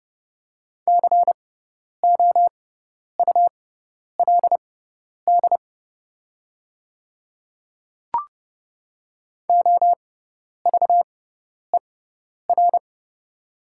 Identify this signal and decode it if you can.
Morse code, heard as a keyed tone, about 30 wpm, spaced as for 10 wpm, 700 Hz. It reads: COULD OVER
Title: 6x Character Spacing - Top 100 Words - 30wpm